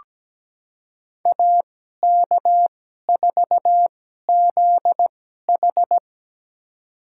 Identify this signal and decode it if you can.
Morse code, heard as a keyed tone, about 17 wpm, spaced as for 12 wpm, 700 Hz. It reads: AK4ZH
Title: USA Call Signs - 5 Characters - 2x Character Spacing - 17wpm